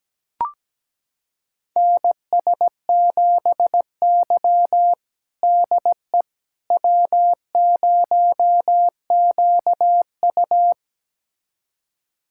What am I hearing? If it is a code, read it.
Morse code, heard as a keyed tone, about 17 wpm, 700 Hz. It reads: NS7Y DE W0QU
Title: USA Callsign DE Callsign - 4 Characters - 17wpm